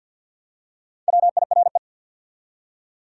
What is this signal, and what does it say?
Morse code, heard as a keyed tone, about 53 wpm, 700 Hz. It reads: WIRE
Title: Top 180 QSO Elements 50wpm